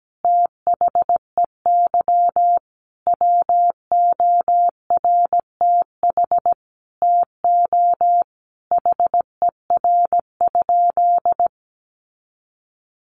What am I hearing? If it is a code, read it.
Morse code, heard as a keyed tone, about 17 wpm, 700 Hz. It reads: THEY WORTH TO HER?
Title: Sentences from Top 800 Words 17wpm